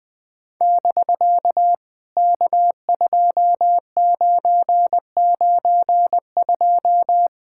Morse code, heard as a keyed tone, about 20 wpm, 700 Hz. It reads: <BK> K2992
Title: POTA QSO Practice - 20wpm